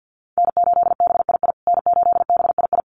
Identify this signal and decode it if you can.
Morse code, heard as a keyed tone, about 48 wpm, 700 Hz. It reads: N8BII N8BII
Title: USA Call Signs Repeated 2x 50wpm